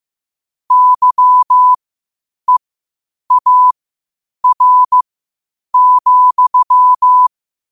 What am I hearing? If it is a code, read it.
Morse code, heard as a keyed tone, about 15 wpm, spaced as for 8 wpm, 1000 Hz. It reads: YEAR,
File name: qotdfiles / 8wpm